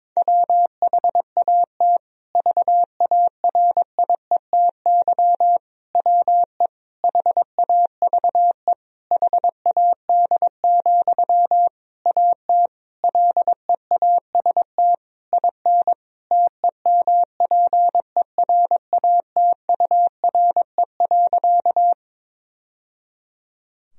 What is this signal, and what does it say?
Morse code, heard as a keyed tone, about 22 wpm, 700 Hz. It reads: WHAT VARIETY WE HAVE HAD, AT LEAST IN TEMPERATURE.
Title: Sentences from Top 1000 Words - Review Part 4 - 22wpm